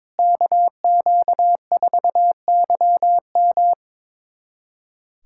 Morse code, heard as a keyed tone, about 22 wpm, 700 Hz. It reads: KQ4YM
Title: USA Callsign DE Callsign 22wpm